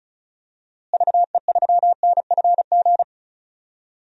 Morse code, heard as a keyed tone, about 34 wpm, 700 Hz. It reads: VE3NFG
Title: DXCC-100 Call Signs - 6 Characters - Repeated 3x - 35wpm